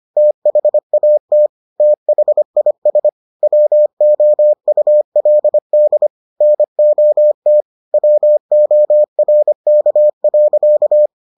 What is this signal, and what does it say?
Morse code, heard as a keyed tone, about 25 wpm, 600 Hz. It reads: THAT THIS WOULD NOT WORK.